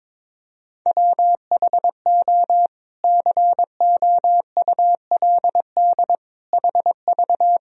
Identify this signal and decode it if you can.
Morse code, heard as a keyed tone, about 22 wpm, 700 Hz. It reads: WHO COULD HV
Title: Sentences from Top 300 Words and Abbreviations 22wpm